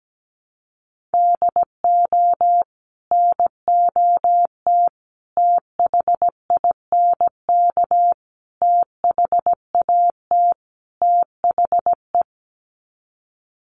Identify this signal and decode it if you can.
Morse code, heard as a keyed tone, about 17 wpm, 700 Hz. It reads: DO NOT THINK THAT THE
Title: Sets of 5 Words Encore2 17wpm